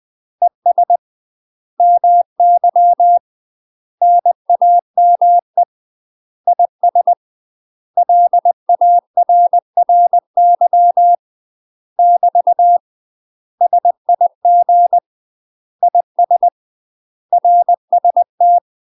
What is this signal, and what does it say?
Morse code, heard as a keyed tone, about 20 wpm, 700 Hz. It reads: ES MY NAME IS LARRY <BT> SIG IS RST
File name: UK MASTER QSOs TRAINING / 20WPM / UK MASTER QSOs 0012